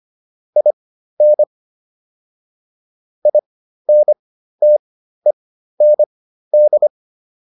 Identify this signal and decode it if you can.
Morse code, heard as a keyed tone, about 25 wpm, spaced as for 13 wpm, 600 Hz. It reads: IN INTEND